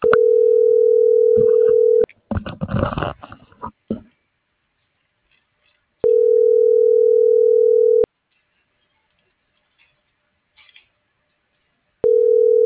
Ringback tone (North America), 440 + 480 Hz.